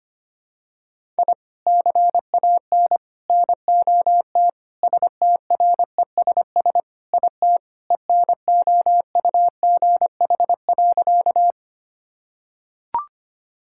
Morse code, heard as a keyed tone, about 25 wpm, 700 Hz. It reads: I CAN NOT STRESS IT ENOUGH.
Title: Sentences from Top 1000 Words - Review Part 6 - 25wpm